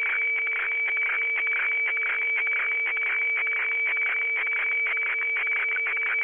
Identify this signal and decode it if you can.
Weather fax (radiofax).